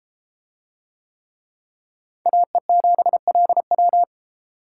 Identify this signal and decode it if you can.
Morse code, heard as a keyed tone, about 30 wpm, 700 Hz. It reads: AE7LW